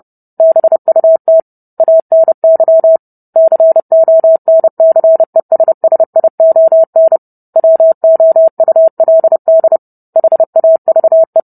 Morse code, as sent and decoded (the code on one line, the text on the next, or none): -... ..- - / .- -. -.-- / -.-. --- -. -.-. ... ... .. --- -. / .-- --- ..- .-.. -.. / .... .- ...- .
BUT ANY CONCESSION WOULD HAVE